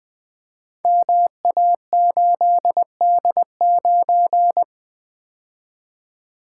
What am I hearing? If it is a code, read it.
Morse code, heard as a keyed tone, about 20 wpm, 700 Hz. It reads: MA8D9
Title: Code Groups - 5 Letter:Numbers - 20wpm